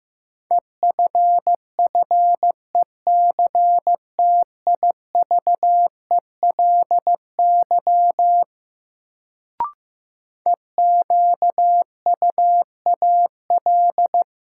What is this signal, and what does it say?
Morse code, heard as a keyed tone, about 15 wpm, 700 Hz. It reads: EFFECTIVELY EQUAL
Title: Sentences from Top 1400 Words - New Word Drill - 15wpm